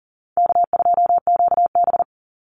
Morse code, heard as a keyed tone, about 40 wpm, 700 Hz. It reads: K2QB
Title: Easy Call Signs Repeated 3x - 40wpm